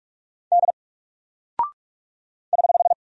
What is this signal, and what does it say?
Morse code, heard as a keyed tone, about 45 wpm, 700 Hz. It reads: D <HH>
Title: Single Character (Letter:Number:Punctuation:Prosign) Alt 45wpm